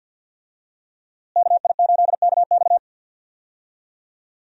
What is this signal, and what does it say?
Morse code, heard as a keyed tone, about 50 wpm, 700 Hz. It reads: KI8KX
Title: USA Call Signs - 5 Characters - Repeated 3x - 50wpm